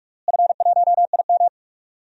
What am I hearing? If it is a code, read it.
Morse code, heard as a keyed tone, about 44 wpm, 700 Hz. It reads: F1IM
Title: DXCC-100 Call Signs - 4 Characters - Repeated 3x - 45wpm